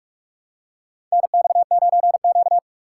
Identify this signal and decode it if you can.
Morse code, heard as a keyed tone, about 44 wpm, 700 Hz. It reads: NX9Q
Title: USA Call Signs - 6x Character Spacing - 45wpm